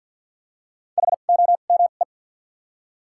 Morse code, heard as a keyed tone, about 53 wpm, spaced as for 37 wpm, 700 Hz. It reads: HOME
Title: 2x Character Spacing - Top 200 Words - 50wpm